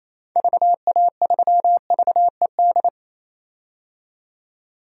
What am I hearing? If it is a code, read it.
Morse code, heard as a keyed tone, about 28 wpm, 700 Hz. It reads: VA3VED
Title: DXCC-100 Call Signs - 6 Characters - 28wpm